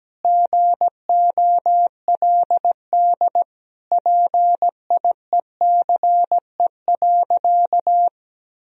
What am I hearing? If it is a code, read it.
Morse code, heard as a keyed tone, about 17 wpm, 700 Hz. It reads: GOLD PIECE.
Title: Sentences from Top 1000 Words - Review Part 3 - 17wpm